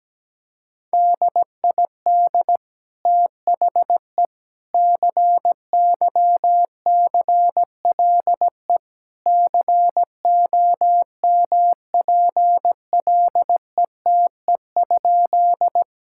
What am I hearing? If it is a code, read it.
Morse code, heard as a keyed tone, about 17 wpm, 700 Hz. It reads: DID THE CYCLE COMPLETE?
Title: Sentences from Top 1000 Words - Review Part 6 - 17wpm